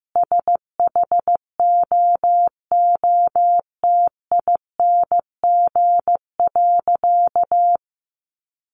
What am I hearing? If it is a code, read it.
Morse code, heard as a keyed tone, about 15 wpm, 700 Hz. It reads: SHOOTING.
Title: A Sound Of Thunder 15wpm